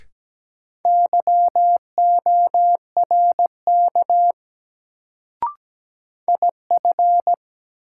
Morse code, heard as a keyed tone, about 17 wpm, 700 Hz. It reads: YORK IF
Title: Sets of 2 Words - New Word Drill - 17wpm